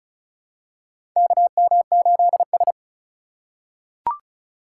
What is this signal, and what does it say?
Morse code, heard as a keyed tone, about 34 wpm, 700 Hz. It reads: KM8S
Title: USA Call Signs - 4 Characters - 2x Character Spacing - 35wpm